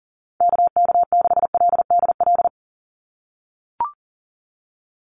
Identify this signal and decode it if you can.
Morse code, heard as a keyed tone, about 40 wpm, 700 Hz. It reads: KK6LDL
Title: USA Call Signs 40wpm